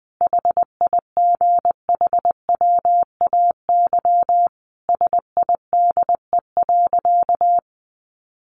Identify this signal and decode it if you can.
Morse code, heard as a keyed tone, about 20 wpm, 700 Hz. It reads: HIGHWAY SIDE.
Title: Sentences from Top 1400 Words 20wpm